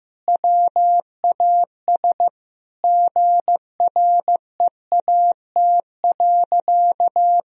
Morse code, heard as a keyed tone, about 15 wpm, 700 Hz. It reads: WAS GREAT.
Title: Sentences from Top 1000 Words - Review Part 4 - 15wpm